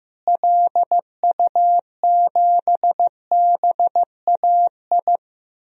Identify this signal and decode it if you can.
Morse code, heard as a keyed tone, about 16 wpm, 700 Hz. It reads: LU7BAI
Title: DXCC-100 Call Signs - 6 Characters - Repeated 2x - 15wpm